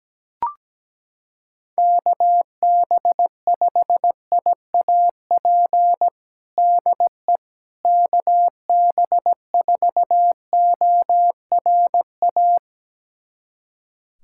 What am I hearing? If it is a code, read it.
Morse code, heard as a keyed tone, about 17 wpm, 700 Hz. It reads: KB5IAP DE KB4ORA
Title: USA Callsign DE Callsign - 6 Characters - 17wpm